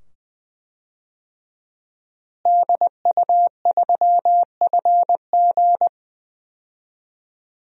Morse code, heard as a keyed tone, about 21 wpm, 700 Hz. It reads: DU3FG
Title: DXCC-100 Call Signs - 5 Characters - Repeated 2x - 20wpm